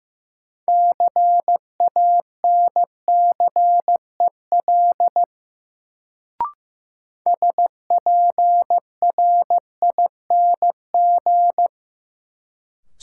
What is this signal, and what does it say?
Morse code, heard as a keyed tone, about 15 wpm, 700 Hz. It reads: CANCEL SPRING
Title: Sentences from Top 1200 Words - New Word Drill Alt - 15wpm